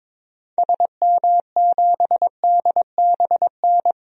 Morse code, heard as a keyed tone, about 22 wpm, 700 Hz. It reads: SM7DBN
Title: DXCC-100 Call Signs - 6 Characters - Repeated 3x - 22wpm